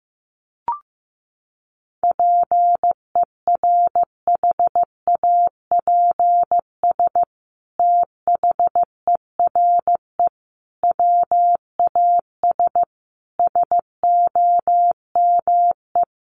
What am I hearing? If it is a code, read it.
Morse code, heard as a keyed tone, about 15 wpm, 700 Hz. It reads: PERHAPS THERE WAS SOME
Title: Sentences from Top 700 Words 15wpm